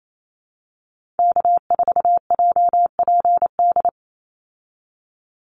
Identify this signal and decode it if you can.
Morse code, heard as a keyed tone, about 28 wpm, 700 Hz. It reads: K4JPD
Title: Call Signs Repeated 2x - 28wpm